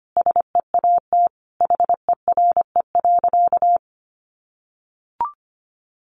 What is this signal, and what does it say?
Morse code, heard as a keyed tone, about 25 wpm, 700 Hz. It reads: SEAT HERE.
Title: Sentences from Top 1500 Words 25wpm